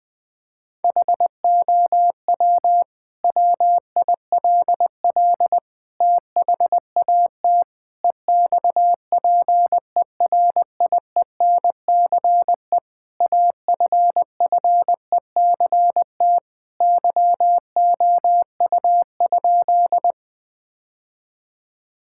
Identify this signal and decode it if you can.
Morse code, heard as a keyed tone, about 19 wpm, 700 Hz. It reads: HOW WILL THAT EXPERIENCE AFFECT YOU?
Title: Sentences from Top 1000 Words - Review Part 6 - 20wpm